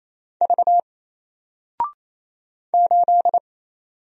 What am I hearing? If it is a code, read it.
Morse code, heard as a keyed tone, about 28 wpm, 700 Hz. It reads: V 8
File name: Single-Letter-Number-28wpm